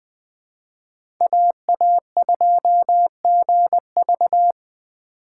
Morse code, heard as a keyed tone, about 20 wpm, 700 Hz. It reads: AA2GV
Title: USA Call Signs - 5 Characters - 4x Character Spacing - 20wpm